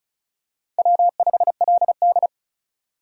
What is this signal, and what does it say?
Morse code, heard as a keyed tone, about 36 wpm, 700 Hz. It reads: W5LD
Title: USA Call Signs - 2x Character Spacing - 35wpm